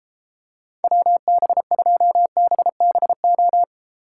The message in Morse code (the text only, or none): WB2BBO